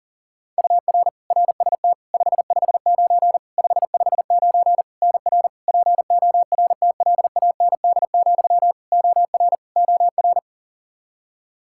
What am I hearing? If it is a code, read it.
Morse code, heard as a keyed tone, about 37 wpm, 700 Hz. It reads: UR RST 559 559 NR PORTLAND, OR OR
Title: Practice QSO - Exchange 1 - Beginning - 40wpm